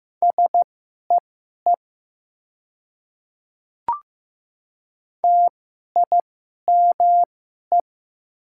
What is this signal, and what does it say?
Morse code, heard as a keyed tone, about 15 wpm, spaced as for 11 wpm, 700 Hz. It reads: SEE TIME
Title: 2x Character Spacing - Top 300 Words - 15wpm